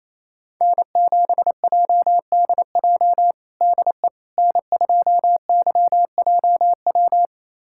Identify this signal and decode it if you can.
Morse code, heard as a keyed tone, about 28 wpm, 700 Hz. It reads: N7JDJ DE N2YJW
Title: USA Callsign DE Callsign 28wpm